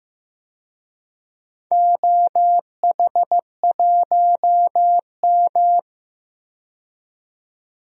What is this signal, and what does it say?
Morse code, heard as a keyed tone, about 15 wpm, 700 Hz. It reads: OH1M
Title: DXCC-100 Call Signs - 4 Characters - 15wpm